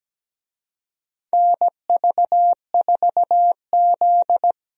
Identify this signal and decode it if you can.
Morse code, heard as a keyed tone, about 17 wpm, 700 Hz. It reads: NV4Z
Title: USA Call Signs - 4 Characters - 17wpm